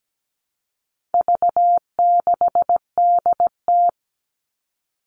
Morse code, heard as a keyed tone, about 17 wpm, 700 Hz. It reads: V6DT
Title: All Cut Numbers - 4-Digit 17wpm